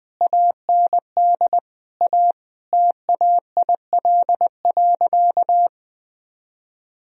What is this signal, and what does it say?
Morse code, heard as a keyed tone, about 20 wpm, 700 Hz. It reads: AND A TAIL.
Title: Sentences from Top 1000 Words - Review Part 3 - 20wpm